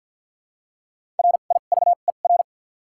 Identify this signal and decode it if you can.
Morse code, heard as a keyed tone, about 52 wpm, spaced as for 37 wpm, 700 Hz. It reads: RIVER
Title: 2x Character Spacing - Top 500 Words - 50wpm